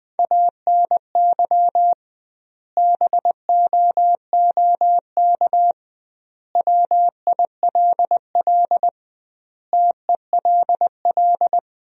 Morse code, heard as a keyed tone, about 20 wpm, 700 Hz. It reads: ANY BOOK WILL TELL
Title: 2x Word-Spacing - Sentences from Top 300 Words - 20wpm